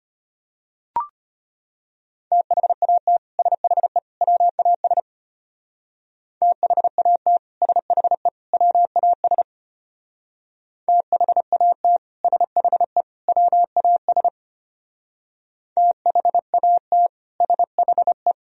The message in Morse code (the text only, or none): THAT SHE WAS THAT SHE WAS THAT SHE WAS THAT SHE